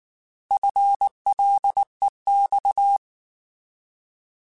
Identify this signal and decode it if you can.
Morse code, heard as a keyed tone, about 19 wpm, 800 Hz. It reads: FLEX
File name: Alphabetical / Alphabetical mp3s / Fs @18